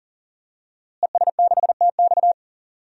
Morse code, heard as a keyed tone, about 40 wpm, 700 Hz. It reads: ES6TX
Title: DXCC-100 Call Signs - 5 Characters - Repeated 2x - 40wpm